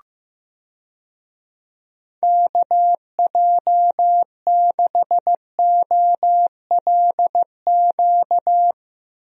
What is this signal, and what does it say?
Morse code, heard as a keyed tone, about 15 wpm, 700 Hz. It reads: KJ6OLQ
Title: USA Call Signs - 6 Characters - Repeated 2x - 15wpm